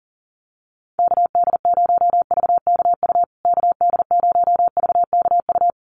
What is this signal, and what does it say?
Morse code, heard as a keyed tone, about 40 wpm, 700 Hz. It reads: KD0VKU KD0VKU
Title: USA Call Signs Repeated 2x 40wpm